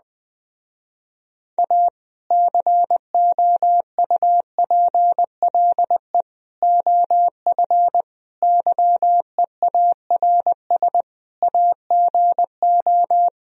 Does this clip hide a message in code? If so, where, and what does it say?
Yes: Morse code from 1.6 to 13.3 s, heard as a keyed tone, about 20 wpm, 700 Hz. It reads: A COUPLE OF YEARS AGO